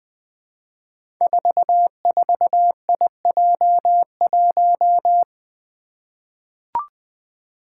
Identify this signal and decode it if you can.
Morse code, heard as a keyed tone, about 20 wpm, 700 Hz. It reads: 44IJ1
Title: Five Challenging Letter-Number Combinations 20wpm